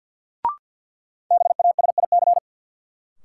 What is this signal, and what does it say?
Morse code, heard as a keyed tone, about 53 wpm, 700 Hz. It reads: BASIC
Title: Sentences from Top 1000 Words - Review Part 3 - New Word Drill - 50wpm